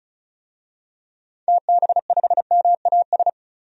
Morse code, heard as a keyed tone, about 36 wpm, 700 Hz. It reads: TB5MAS
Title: DXCC-100 Call Signs - 6 Characters - 35wpm